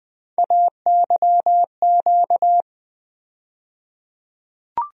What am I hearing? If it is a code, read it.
Morse code, heard as a keyed tone, about 20 wpm, 700 Hz. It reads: AYQ
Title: USA Call Sign Postfixes - 3 Letters 20wpm